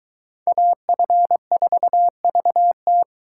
Morse code, heard as a keyed tone, about 23 wpm, 700 Hz. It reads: AF4VT